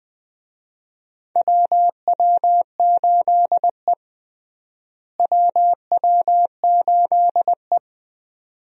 Morse code, heard as a keyed tone, about 20 wpm, 700 Hz. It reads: WW8E WW8E
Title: USA Call Signs - 4 Characters - Repeated 2x - 20wpm